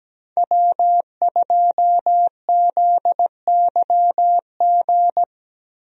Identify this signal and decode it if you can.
Morse code, heard as a keyed tone, about 17 wpm, 700 Hz. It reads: W2ZYG
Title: USA Call Signs - 5 Characters - Repeated 2x - 17wpm